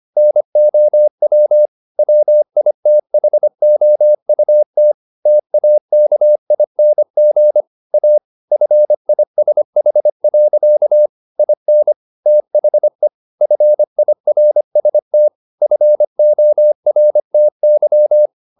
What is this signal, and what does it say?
Morse code, heard as a keyed tone, about 25 wpm, 600 Hz. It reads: NOW WITHOUT TAKING A FISH. IN THE FIRST FORTY